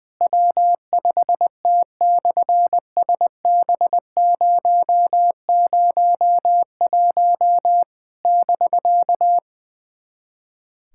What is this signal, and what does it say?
Morse code, heard as a keyed tone, about 20 wpm, 700 Hz. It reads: W5T/SB001 <BK>
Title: SOTA QSO Practice 20wpm